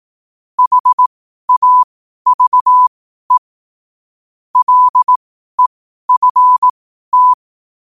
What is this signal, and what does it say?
Morse code, heard as a keyed tone, about 18 wpm, spaced as for 12 wpm, 1000 Hz. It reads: HAVE LEFT
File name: qotdfiles / 12wpm